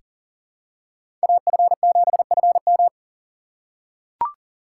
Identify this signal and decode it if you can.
Morse code, heard as a keyed tone, about 40 wpm, 700 Hz. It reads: AF7FM
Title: USA Call Signs - 5 Characters - 4x Character Spacing - 40wpm